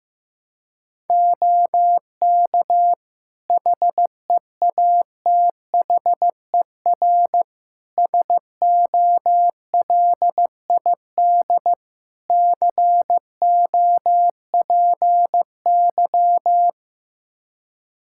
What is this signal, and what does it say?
Morse code, heard as a keyed tone, about 15 wpm, 700 Hz. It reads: OK HEATHER SOLID COPY
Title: K1USN SST QSO Practice 15wpm